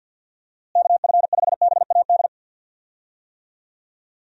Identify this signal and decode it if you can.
Morse code, heard as a keyed tone, about 50 wpm, 700 Hz. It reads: KF5BAD
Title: USA Call Signs - 6 Characters - Repeated 2x - 50wpm